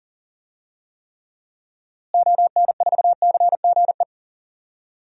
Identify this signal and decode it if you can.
Morse code, heard as a keyed tone, about 40 wpm, 700 Hz. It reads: ON4CGE
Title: DXCC-100 Call Signs - 6 Characters - Repeated 2x - 40wpm